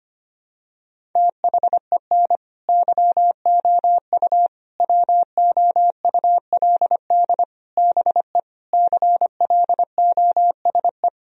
Morse code, heard as a keyed tone, about 25 wpm, 700 Hz. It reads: THEN YOU WOULD BE CLOSE
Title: Sentences from Top 1000 Words - Review Part 1 - 25wpm